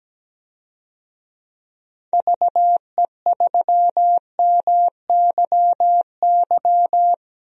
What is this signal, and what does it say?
Morse code, heard as a keyed tone, about 17 wpm, 700 Hz. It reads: VE3MYY
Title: DXCC-100 Call Signs - 6 Characters - 17wpm